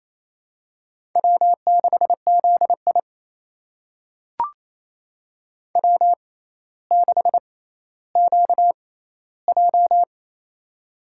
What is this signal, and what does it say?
Morse code, heard as a keyed tone, about 28 wpm, spaced as for 14 wpm, 700 Hz. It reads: W6ZI W6QJ
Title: USA Call Signs - 4 Characters - 6x Character Spacing - 28wpm